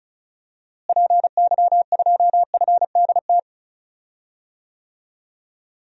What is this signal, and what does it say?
Morse code, heard as a keyed tone, about 36 wpm, 700 Hz. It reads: PY2FDT
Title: DXCC-100 Call Signs - 6 Characters - Repeated 2x - 35wpm